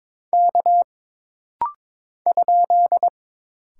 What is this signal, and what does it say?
Morse code, heard as a keyed tone, about 22 wpm, 700 Hz. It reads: K ?
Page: This rapid-fire timing is geared to push your ICR (Instant Character Recognition) skills.